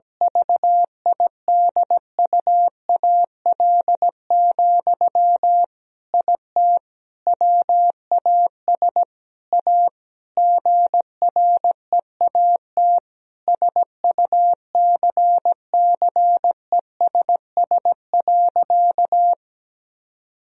Morse code, heard as a keyed tone, about 17 wpm, 700 Hz. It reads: VIDUAL, IT WAS A GREAT SUCCESS.
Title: Sentences from Top 1000 Words - Review Part 4 - 17wpm